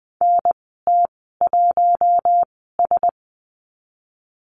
Morse code, heard as a keyed tone, about 20 wpm, spaced as for 15 wpm, 700 Hz. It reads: NT1S
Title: 048 Lesson 9g - Number 4 - Call Signs 20wpm